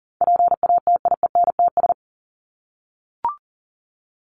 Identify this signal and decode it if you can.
Morse code, heard as a keyed tone, about 40 wpm, 700 Hz. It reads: PATIENTS
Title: ES:S Suffix - Top 100 Words with Base Words 40wpm